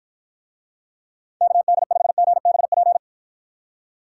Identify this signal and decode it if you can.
Morse code, heard as a keyed tone, about 53 wpm, 700 Hz. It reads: KD5GBP